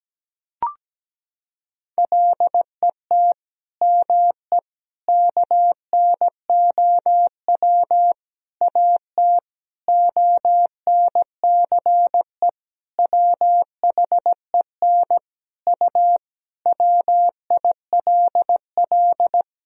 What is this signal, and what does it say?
Morse code, heard as a keyed tone, about 17 wpm, 700 Hz. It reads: LET ME KNOW AT ONCE WHEN U WILL
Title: Sentences from Top 300 Words and Abbreviations 17wpm